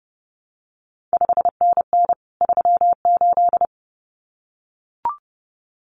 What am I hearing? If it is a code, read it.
Morse code, heard as a keyed tone, about 30 wpm, 700 Hz. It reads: 5NN 38
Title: CQ WW DX Contest Exchange Encore - 30wpm